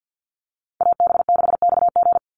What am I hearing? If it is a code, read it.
Morse code, heard as a keyed tone, about 50 wpm, 700 Hz. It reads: AB6XG
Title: Call Signs - 50wpm